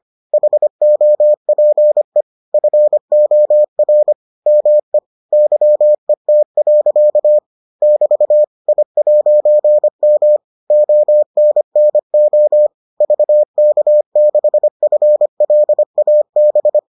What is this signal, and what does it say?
Morse code, heard as a keyed tone, about 25 wpm, 600 Hz. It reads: HOPE FOR ME YET. <BT> I'M ONNO VK6FLAB